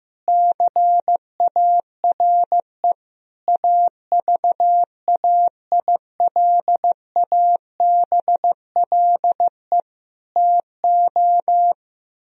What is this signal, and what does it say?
Morse code, heard as a keyed tone, about 15 wpm, 700 Hz. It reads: CARE AVAILABLE TO
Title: Sentences from Top 1000 Words - Review Part 3 - 15wpm